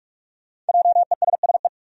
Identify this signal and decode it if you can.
Morse code, heard as a keyed tone, about 46 wpm, 700 Hz. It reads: JESSE
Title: USA Common Names - 45wpm